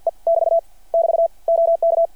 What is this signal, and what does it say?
Morse code, heard as a keyed tone, about 53 wpm, 690 Hz. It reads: E <BT> <BT> OK
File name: archived / CRSnet